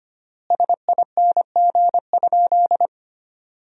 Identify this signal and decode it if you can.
Morse code, heard as a keyed tone, about 25 wpm, 700 Hz. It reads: SING?
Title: 3x Word-Spacing - Sentences from Top 400 Words - 25wpm